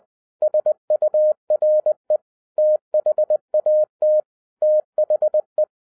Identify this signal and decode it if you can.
Morse code, heard as a keyed tone, about 20 wpm, 610 Hz. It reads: SURE THAT THE